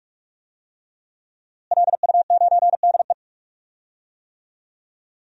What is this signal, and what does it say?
Morse code, heard as a keyed tone, about 46 wpm, 700 Hz. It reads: LU9DE
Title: DXCC-100 Call Signs - 5 Characters - Repeated 3x - 45wpm